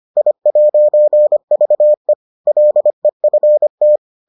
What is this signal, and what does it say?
Morse code, heard as a keyed tone, about 25 wpm, 600 Hz. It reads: I'VE LEFT